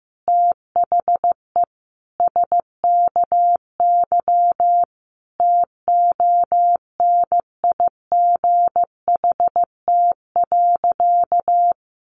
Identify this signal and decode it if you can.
Morse code, heard as a keyed tone, about 15 wpm, 700 Hz. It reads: THE SKY TONIGHT.
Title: Sentences from Top 1300 Words 15wpm